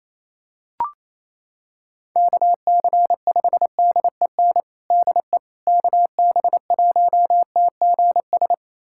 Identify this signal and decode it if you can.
Morse code, heard as a keyed tone, about 28 wpm, 700 Hz. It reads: KC5DEN DE KB1TGS
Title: USA Callsign DE Callsign - 6 Characters - 28wpm